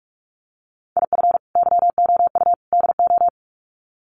Morse code, heard as a keyed tone, about 44 wpm, 700 Hz. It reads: IF YOU DO
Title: Sets of 3 Words 45wpm